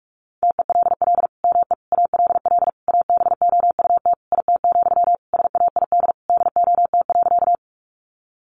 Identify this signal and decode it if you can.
Morse code, heard as a keyed tone, about 44 wpm, 700 Hz. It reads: TELL ME ALL ABOUT IT, SAID DOT.
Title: Sentences from Top 1500 Words 45wpm